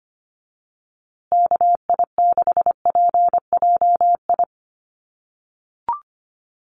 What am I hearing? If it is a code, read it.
Morse code, heard as a keyed tone, about 25 wpm, 700 Hz. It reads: KI6PJI
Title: USA Call Signs Repeated 3x 25wpm